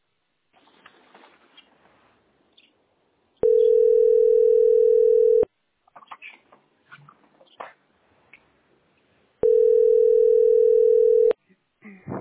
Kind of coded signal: telephone call progress tone